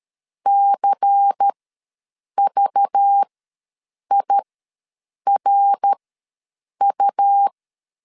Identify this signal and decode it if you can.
Morse code, heard as a keyed tone, about 13 wpm, spaced as for 7 wpm, 800 Hz. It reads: CVIRU
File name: LEZIONI / 7 GF9Z / 7 G ws 12wpm 3140